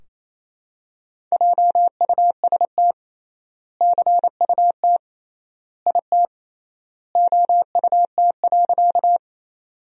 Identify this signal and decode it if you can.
Morse code, heard as a keyed tone, about 27 wpm, 700 Hz. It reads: JUST CUT IT OUT.
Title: 3x Word-Spacing - Sentences from Top 300 Words - 28wpm